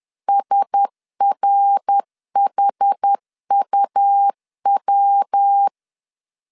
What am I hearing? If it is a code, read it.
Morse code, heard as a keyed tone, about 10 wpm, 800 Hz. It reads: SRHUW
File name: CW / DET / 1 6 1500C S B 10wpm 4137